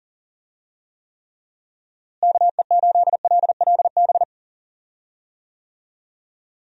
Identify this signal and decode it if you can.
Morse code, heard as a keyed tone, about 40 wpm, 700 Hz. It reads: KE8LLB
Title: USA Call Signs - 6 Characters - Repeated 2x - 40wpm